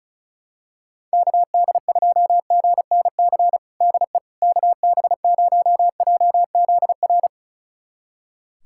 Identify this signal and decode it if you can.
Morse code, heard as a keyed tone, about 34 wpm, 700 Hz. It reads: KD2GNC DE KB0JZR
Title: USA Callsign DE Callsign - 6 Characters - 35wpm